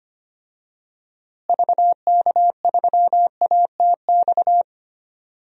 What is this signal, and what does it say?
Morse code, heard as a keyed tone, about 25 wpm, 700 Hz. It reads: VK3ATX
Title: DXCC-100 Call Signs - 6 Characters - Repeated 3x - 25wpm